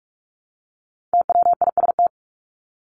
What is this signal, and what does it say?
Morse code, heard as a keyed tone, about 44 wpm, 700 Hz. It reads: TWIST
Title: Top 1751-2000 Words 45wpm